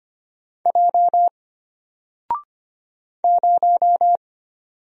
Morse code, heard as a keyed tone, about 25 wpm, 700 Hz. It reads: J 0
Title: Single Letter-Number Alt 25wpm